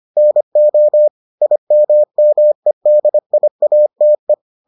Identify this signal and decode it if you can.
Morse code, heard as a keyed tone, about 25 wpm, 600 Hz. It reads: NO IMMEDIATE